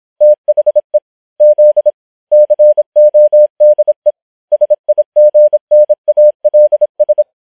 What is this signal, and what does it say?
Morse code, heard as a keyed tone, about 26 wpm, 600 Hz. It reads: THE Z CODE SIGNALS
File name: challenge193